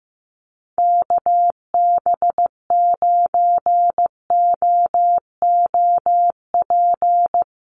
Morse code, heard as a keyed tone, about 15 wpm, 700 Hz. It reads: KB9OOP
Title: USA Call Signs Repeated 3x 15wpm